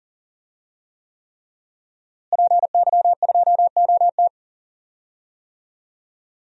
Morse code, heard as a keyed tone, about 40 wpm, 700 Hz. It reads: PY2OT
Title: DXCC-100 Call Signs - 5 Characters - Repeated 3x - 40wpm